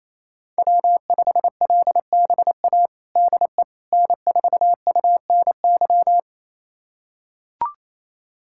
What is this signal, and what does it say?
Morse code, heard as a keyed tone, about 28 wpm, 700 Hz. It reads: W5LBA DE N4UNY
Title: USA Callsign DE Callsign - 5 Characters - 28wpm